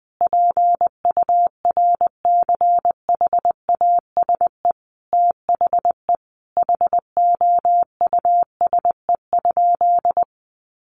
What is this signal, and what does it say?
Morse code, heard as a keyed tone, about 20 wpm, 700 Hz. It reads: PURCHASE THE HOUSE?
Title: Sentences from Top 800 Words 20wpm